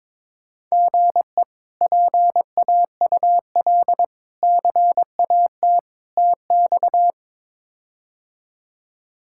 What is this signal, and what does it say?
Morse code, heard as a keyed tone, about 22 wpm, 700 Hz. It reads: GE PAUL CAT TX
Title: K1USN SST QSO Practice - 22wpm